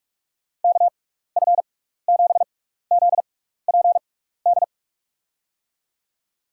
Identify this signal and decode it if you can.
Morse code, heard as a keyed tone, about 41 wpm, spaced as for 15 wpm, 700 Hz. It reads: KF7ZPD
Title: USA Call Signs - 6 Characters - 6x Character Spacing - 45wpm